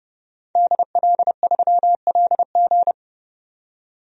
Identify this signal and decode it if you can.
Morse code, heard as a keyed tone, about 31 wpm, 700 Hz. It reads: DL3LG
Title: DXCC-100 Call Signs - 5 Characters - 30wpm